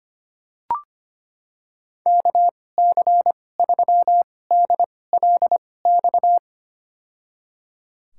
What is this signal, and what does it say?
Morse code, heard as a keyed tone, about 25 wpm, spaced as for 18 wpm, 700 Hz. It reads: KC3DLX
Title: USA Call Signs - 6 Characters - 2x Character Spacing - 25wpm